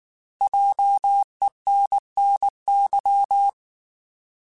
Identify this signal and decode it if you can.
Morse code, heard as a keyed tone, about 19 wpm, 800 Hz. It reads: JENNY
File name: Alphabetical / Alphabetical mp3s / Js @18